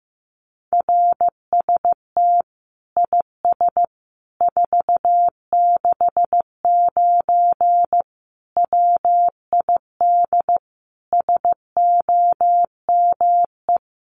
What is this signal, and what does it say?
Morse code, heard as a keyed tone, about 15 wpm, 700 Hz. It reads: RST IS 469 WID SOME
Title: Common QSO Phrases 15wpm